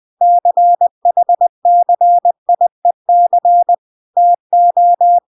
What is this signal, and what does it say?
Morse code, heard as a keyed tone, about 20 wpm, 700 Hz. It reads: CHCIEC TO